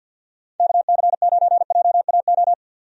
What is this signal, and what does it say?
Morse code, heard as a keyed tone, about 49 wpm, 700 Hz. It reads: KC9JAO
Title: USA Call Signs - 6 Characters - 50wpm